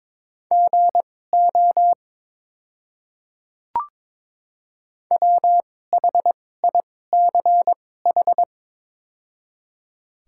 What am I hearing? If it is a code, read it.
Morse code, heard as a keyed tone, about 22 wpm, spaced as for 16 wpm, 700 Hz. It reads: GO WHICH
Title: 2x Character Spacing - Top 100 Words - 22wpm